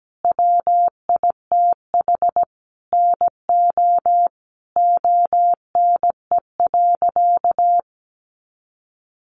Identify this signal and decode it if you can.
Morse code, heard as a keyed tone, about 17 wpm, 700 Hz. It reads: WITH NO ONE.
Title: Additional Easy Phrases 17wpm